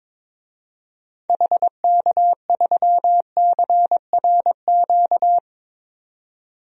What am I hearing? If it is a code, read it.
Morse code, heard as a keyed tone, about 22 wpm, 700 Hz. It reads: HK3CRQ
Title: DXCC-100 Call Signs - 6 Characters - 22wpm